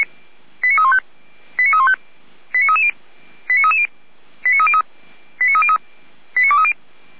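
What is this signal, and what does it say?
Five-tone selective call (ZVEI), heard as quick runs of five tones, tones of about 80 ms. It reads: 89315 89315 89330 89330 89303 89303 89230
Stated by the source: ZVEI five-tone selective call received on an SDR.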